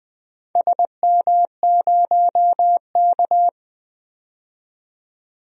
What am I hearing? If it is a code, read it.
Morse code, heard as a keyed tone, about 20 wpm, 700 Hz. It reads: SM0K
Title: DXCC-100 Call Signs - 4 Characters - 20wpm